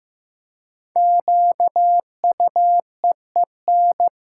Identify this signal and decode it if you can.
Morse code, heard as a keyed tone, about 15 wpm, 700 Hz. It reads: QUEEN
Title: Five Letter Words - 15wpm